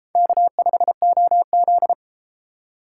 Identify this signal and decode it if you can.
Morse code, heard as a keyed tone, about 30 wpm, 700 Hz. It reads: K5OZ